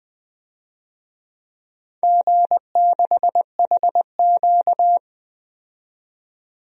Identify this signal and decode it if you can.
Morse code, heard as a keyed tone, about 20 wpm, 700 Hz. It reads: G6HQ
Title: DXCC-100 Call Signs - 4 Characters - 20wpm